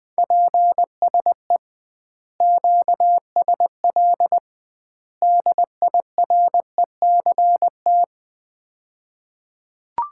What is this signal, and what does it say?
Morse code, heard as a keyed tone, about 20 wpm, 700 Hz. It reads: PSE QSL DIRECT
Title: Common QSO Phrases - 2x Word-Spacing - 20wpm